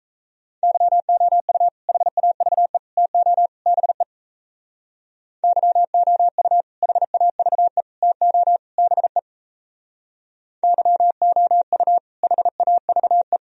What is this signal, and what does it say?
Morse code, heard as a keyed tone, about 34 wpm, 700 Hz. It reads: YOU HAVE TO BE YOU HAVE TO BE YOU HAVE
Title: Speed-Racing - Sets of 4 Words 28wpm